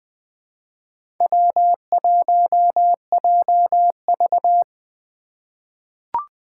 Morse code, heard as a keyed tone, about 20 wpm, 700 Hz. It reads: W1JV
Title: USA Call Signs - 4 Characters - 2x Character Spacing - 20wpm